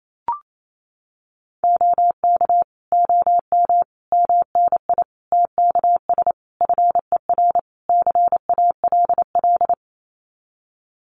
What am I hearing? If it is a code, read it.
Morse code, heard as a keyed tone, about 28 wpm, 700 Hz. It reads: OK OM MNI TKS FER CALL
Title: Common QSO Phrases 28wpm